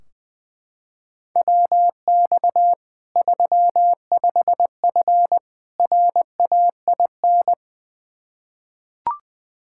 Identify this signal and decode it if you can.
Morse code, heard as a keyed tone, about 20 wpm, 700 Hz. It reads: WX 35F RAIN
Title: K1USN SST QSO Practice 20wpm